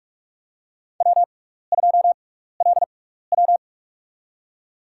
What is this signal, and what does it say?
Morse code, heard as a keyed tone, about 46 wpm, spaced as for 16 wpm, 700 Hz. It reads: W2LW
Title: USA Call Signs - 4 Characters - 6x Character Spacing - 45wpm